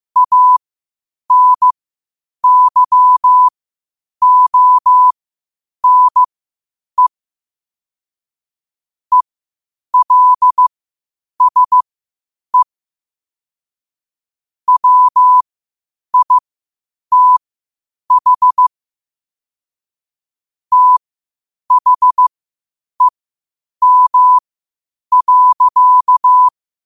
Morse code, heard as a keyed tone, about 15 wpm, spaced as for 8 wpm, 1000 Hz. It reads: ANYONE ELSE WITH THEM.
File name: qotdfiles / 8wpm